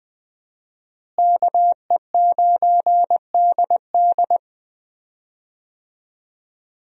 Morse code, heard as a keyed tone, about 20 wpm, 700 Hz. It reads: KE9DD
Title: USA Call Signs - 5 Characters - Repeated 3x - 20wpm